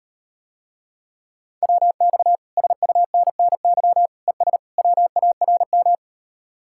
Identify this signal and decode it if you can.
Morse code, heard as a keyed tone, about 38 wpm, 700 Hz. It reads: WX SUNNY ES WARM